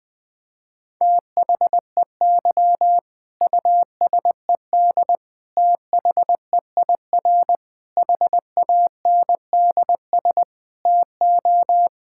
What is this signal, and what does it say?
Morse code, heard as a keyed tone, about 20 wpm, 700 Hz. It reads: THEY USED THEIR HANDS TO
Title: Sentences from Top 1000 Words - Review Part 6 - 20wpm